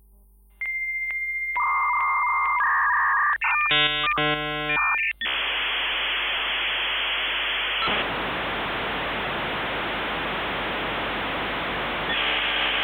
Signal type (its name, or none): dial-up modem handshake